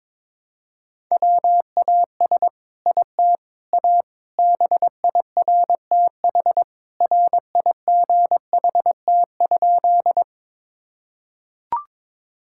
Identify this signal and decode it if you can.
Morse code, heard as a keyed tone, about 21 wpm, 700 Hz. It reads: WAS IT A BIRTH RIGHT?
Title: Sentences from Top 1000 Words - Review Part 5 - 22wpm